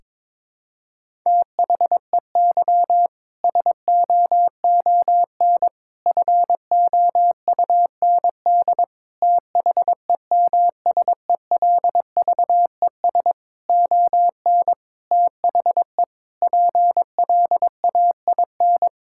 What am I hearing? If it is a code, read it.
Morse code, heard as a keyed tone, about 22 wpm, 700 Hz. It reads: THEY SOON FOUND THEMSELVES ON THE PLAIN
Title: Sentences from Top 1000 Words - Review Part 5 - 22wpm